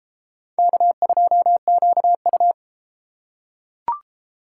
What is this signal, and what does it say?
Morse code, heard as a keyed tone, about 31 wpm, 700 Hz. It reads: K2QU